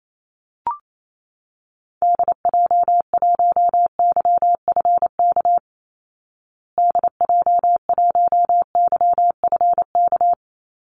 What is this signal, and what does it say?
Morse code, heard as a keyed tone, about 28 wpm, 700 Hz. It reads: DJ1YFK DJ1YFK
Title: Call Signs Repeated 2x - 28wpm